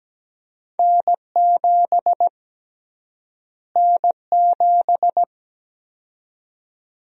Morse code, heard as a keyed tone, about 17 wpm, 700 Hz. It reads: N7 N7
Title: USA Call Sign Prefixes - Repeated 2x - 17wpm